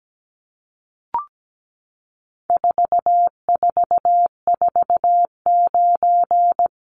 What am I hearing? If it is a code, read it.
Morse code, heard as a keyed tone, about 17 wpm, 700 Hz. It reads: 4449
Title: Numbers - 4-Digit 17wpm